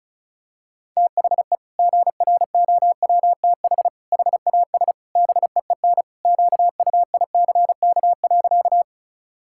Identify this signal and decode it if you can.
Morse code, heard as a keyed tone, about 34 wpm, 700 Hz. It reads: THE GROWTH HAS BEEN QUICK.
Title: Sentences from Top 1000 Words - Review Part 5 - 35wpm